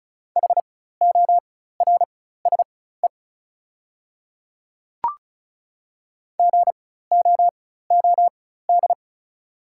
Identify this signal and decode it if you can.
Morse code, heard as a keyed tone, about 36 wpm, spaced as for 16 wpm, 690 Hz. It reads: HORSE GOOD
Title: 4x Character Spacing - Top 300 Words - 35wpm